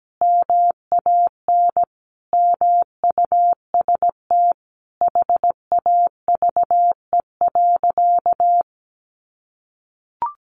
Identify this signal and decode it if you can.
Morse code, heard as a keyed tone, about 17 wpm, 700 Hz. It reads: MAN MUST HAVE.
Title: Sentences from Top 1100 Words 17wpm